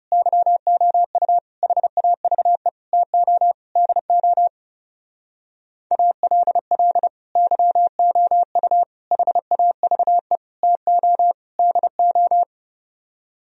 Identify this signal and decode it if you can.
Morse code, heard as a keyed tone, about 39 wpm, 700 Hz. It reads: YOU HAVE TO DO ALL YOU HAVE TO DO